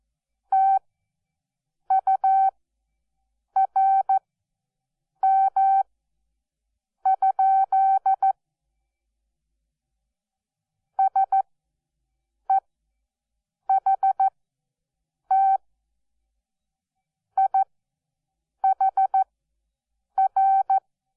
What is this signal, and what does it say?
Morse code, heard as a keyed tone, about 14 wpm, spaced as for 6 wpm, 770 Hz. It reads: TURM? SEHT IHR
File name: Onlinelehrgang / morsen / DJ4UF Morsen12